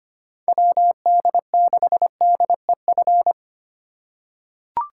Morse code, heard as a keyed tone, about 25 wpm, 700 Hz. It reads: WD6DEF
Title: USA Call Signs - 6 Characters - 3x Character Spacing - 25wpm